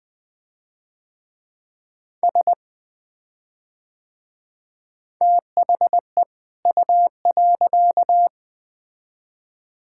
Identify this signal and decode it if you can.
Morse code, heard as a keyed tone, about 20 wpm, 700 Hz. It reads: S THE U.
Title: Build A Phrase - 2-Word Phrases - 20wpm